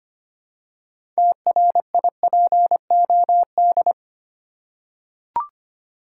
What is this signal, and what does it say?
Morse code, heard as a keyed tone, about 25 wpm, 700 Hz. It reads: TRIPOD
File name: practice / 25wpm / Easy Words with Hints - 25wpm